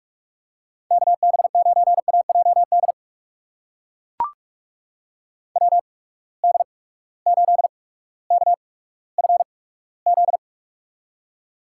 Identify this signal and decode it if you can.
Morse code, heard as a keyed tone, about 42 wpm, spaced as for 19 wpm, 700 Hz. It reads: KB9AJD WD8KFZ
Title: USA Call Signs - 6 Characters - 8x Character Spacing - 45wpm